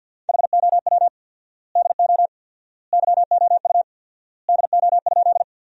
Morse code, heard as a keyed tone, about 50 wpm, 700 Hz. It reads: HOW DO YOU DO?
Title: 4x Word-Spacing - Sentences from Top 100 Words - 50wpm